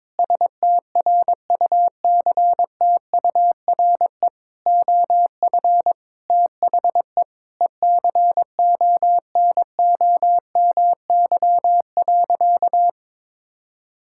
Morse code, heard as a keyed tone, about 22 wpm, 700 Hz. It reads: STRUCTURE OF THE ECONOMY.
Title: Sentences from Top 1000 Words - Review Part 6 - 22wpm